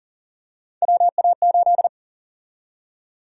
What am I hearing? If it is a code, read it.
Morse code, heard as a keyed tone, about 37 wpm, 700 Hz. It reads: WA8
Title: USA Call Sign Prefixes - 40wpm